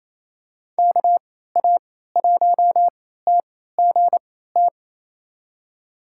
Morse code, heard as a keyed tone, about 27 wpm, spaced as for 16 wpm, 700 Hz. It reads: KA1TGT
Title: USA Call Signs - 6 Characters - 3x Character Spacing - 28wpm